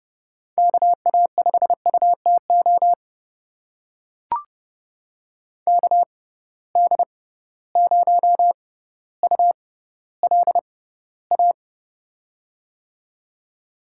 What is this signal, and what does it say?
Morse code, heard as a keyed tone, about 30 wpm, spaced as for 15 wpm, 700 Hz. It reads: KA5UTO KD0ULA
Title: USA Call Signs - 6 Characters - 6x Character Spacing - 30wpm